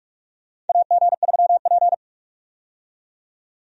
Morse code, heard as a keyed tone, about 46 wpm, 700 Hz. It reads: AG3P
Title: USA Call Signs - 4 Characters - 45wpm